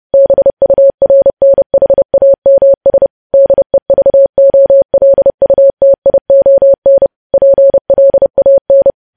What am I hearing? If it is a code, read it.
Morse code, heard as a keyed tone, about 30 wpm, 550 Hz. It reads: BURNHAMS DEVOLUTION PLAN
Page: news-headlines-in-morse-code-at-30-wpm